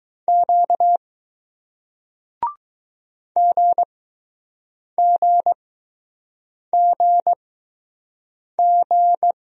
Morse code, heard as a keyed tone, about 20 wpm, spaced as for 6 wpm, 700 Hz. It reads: Q GGGG